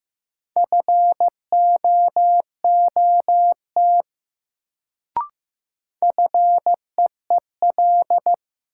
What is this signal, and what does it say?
Morse code, heard as a keyed tone, about 15 wpm, 700 Hz. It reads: FOOT FEEL
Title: Sets of 5 Words - New Word Drill - 15wpm